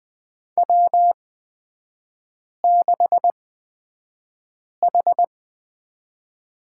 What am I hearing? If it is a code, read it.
Morse code, heard as a keyed tone, about 20 wpm, spaced as for 5 wpm, 690 Hz. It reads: W6H
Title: 229 Lesson 37e - Letter x - Three Characters 20wpm